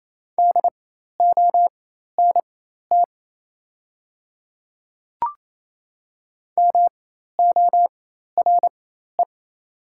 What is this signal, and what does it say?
Morse code, heard as a keyed tone, about 28 wpm, spaced as for 13 wpm, 700 Hz. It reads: DONT MORE
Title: 4x Character Spacing - Top 300 Words - 28wpm